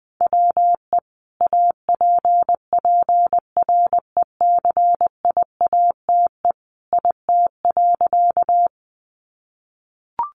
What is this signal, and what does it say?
Morse code, heard as a keyed tone, about 20 wpm, 700 Hz. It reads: WE APPRECIATE IT.
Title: Sentences from Top 1300 Words 20wpm